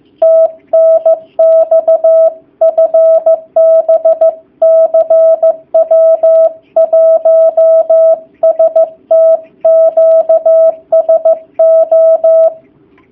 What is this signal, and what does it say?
Morse code, heard as a keyed tone, about 15 wpm, 650 Hz. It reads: TNX FB CW 1ST QSO
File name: jh8cbh / 2019cw / sp15- w